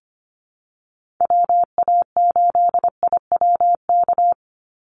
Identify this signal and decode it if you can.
Morse code, heard as a keyed tone, about 25 wpm, 700 Hz. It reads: WA8IWK
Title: Call Signs Repeated 2x - 25wpm